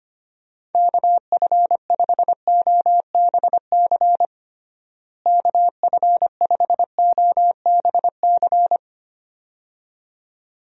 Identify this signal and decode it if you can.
Morse code, heard as a keyed tone, about 25 wpm, 700 Hz. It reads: KF5OBC KF5OBC
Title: USA Call Signs - 6 Characters - Repeated 2x - 25wpm